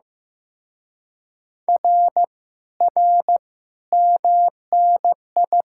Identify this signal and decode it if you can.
Morse code, heard as a keyed tone, about 16 wpm, 700 Hz. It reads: R R MNI
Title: Speed-Racing Common QSO Phrases Encore - 15wpm